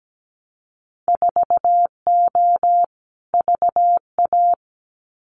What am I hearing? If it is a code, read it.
Morse code, heard as a keyed tone, about 17 wpm, 700 Hz. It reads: 4O VA
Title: Winter Field Day Exchanges Encore 17wpm